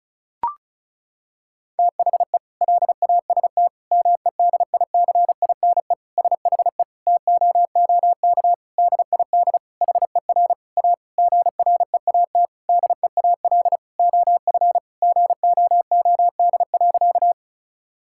Morse code, heard as a keyed tone, about 36 wpm, 700 Hz. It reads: THE LAST MEDICINE SHE TOOK DID HER A GREAT DEAL OF GOOD.
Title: Sentences from Top 1000 Words - Review Part 6 - 35wpm